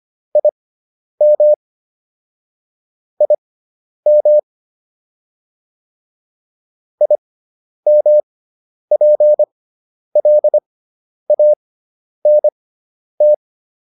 Morse code, heard as a keyed tone, about 25 wpm, spaced as for 10 wpm, 600 Hz. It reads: IM IM IMPLANT